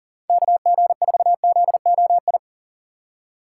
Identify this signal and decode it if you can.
Morse code, heard as a keyed tone, about 40 wpm, 700 Hz. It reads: KG4ZOI
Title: USA Call Signs - 6 Characters - 6x Character Spacing - 40wpm